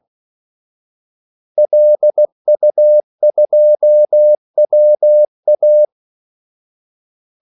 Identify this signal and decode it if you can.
Morse code, heard as a keyed tone, about 16 wpm, 600 Hz. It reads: LU2WA
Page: Kaldesignaler 16 wpm